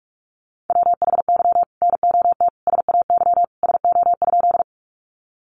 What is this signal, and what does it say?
Morse code, heard as a keyed tone, about 44 wpm, 700 Hz. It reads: WHY NOT SAY SO?
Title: Sentences from Top 600 Words 45wpm